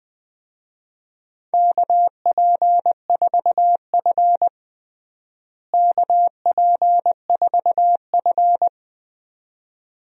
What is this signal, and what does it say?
Morse code, heard as a keyed tone, about 20 wpm, 700 Hz. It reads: KP4F KP4F
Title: DXCC-100 Call Signs - 4 Characters - Repeated 2x - 20wpm